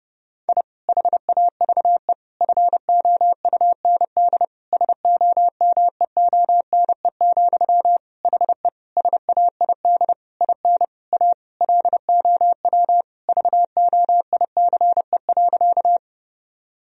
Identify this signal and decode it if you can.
Morse code, heard as a keyed tone, about 30 wpm, 700 Hz. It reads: I HAVE FOUND SOMEONE, HE SAID IN A LOW VOICE.
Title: Sentences from Top 1000 Words - Review Part 1 - 30wpm